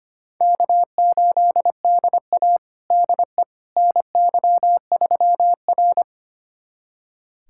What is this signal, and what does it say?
Morse code, heard as a keyed tone, about 25 wpm, 700 Hz. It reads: K8DA DE NY3R
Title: USA Callsign DE Callsign - 4 Characters - 25wpm